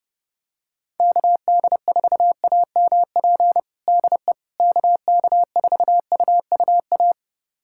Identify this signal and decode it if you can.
Morse code, heard as a keyed tone, about 32 wpm, 700 Hz. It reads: KD4AMP DE KK4UUA
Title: USA Callsign DE Callsign 30wpm